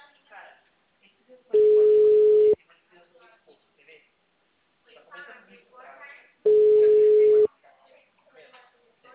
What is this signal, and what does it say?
Ringback tone (Europe, 425 Hz), with speech.
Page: Araras - SP